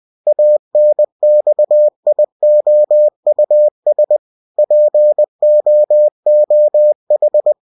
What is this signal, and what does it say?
Morse code, heard as a keyed tone, about 20 wpm, 600 Hz. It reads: ANXIOUS POOH
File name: cw-books / winnie-the-pooh / winnie-20wpm-page-0103